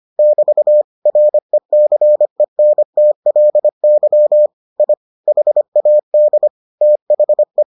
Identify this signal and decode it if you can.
Morse code, heard as a keyed tone, about 25 wpm, 600 Hz. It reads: <BT> RECENTLY I HAD THE